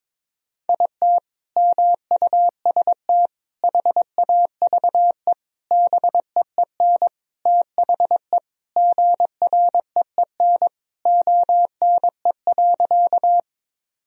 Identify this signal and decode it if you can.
Morse code, heard as a keyed tone, about 22 wpm, 700 Hz. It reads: IT MUST HAVE BEEN THE GREEN ONE.
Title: Sentences from Top 1000 Words - Review Part 2 - 22wpm